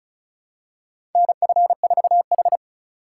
Morse code, heard as a keyed tone, about 36 wpm, 700 Hz. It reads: NF4H
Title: USA Call Signs - 4x Character Spacing - 35wpm